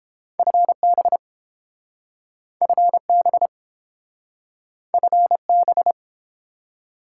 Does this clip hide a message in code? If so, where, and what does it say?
Yes: Morse code from 0.4 to 5.9 s, heard as a keyed tone, about 30 wpm, 700 Hz. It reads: FB FB FB